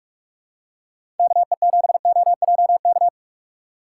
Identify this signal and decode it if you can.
Morse code, heard as a keyed tone, about 44 wpm, 700 Hz. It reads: KE7OJK